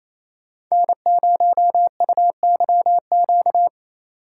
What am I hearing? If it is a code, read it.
Morse code, heard as a keyed tone, about 28 wpm, 700 Hz. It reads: N0UYQ
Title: USA Call Signs - 4x Character Spacing - 28wpm